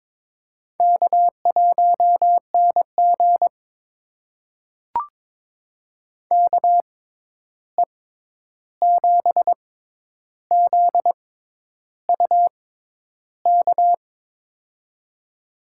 Morse code, heard as a keyed tone, about 21 wpm, spaced as for 10 wpm, 700 Hz. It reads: K1NG KE7ZUK
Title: USA Call Signs - 6x Character Spacing - 22wpm